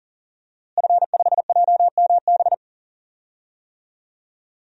Morse code, heard as a keyed tone, about 40 wpm, 700 Hz. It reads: F5JMB
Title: DXCC-100 Call Signs - 5 Characters - Repeated 2x - 40wpm